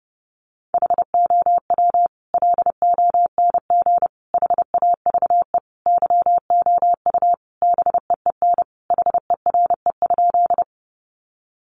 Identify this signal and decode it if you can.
Morse code, heard as a keyed tone, about 28 wpm, 700 Hz. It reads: HOW LONG HAVE YOU BEEN HERE?
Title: Sentences from Top 200 Words 30wpm